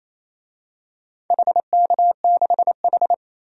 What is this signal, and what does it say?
Morse code, heard as a keyed tone, about 27 wpm, 700 Hz. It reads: HK6H
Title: DXCC-100 Call Signs - 4 Characters - 28wpm